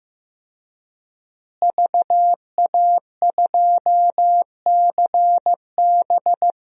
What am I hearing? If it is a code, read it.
Morse code, heard as a keyed tone, about 15 wpm, 700 Hz. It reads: VA2CB
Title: DXCC-100 Call Signs - 5 Characters - Repeated 2x - 15wpm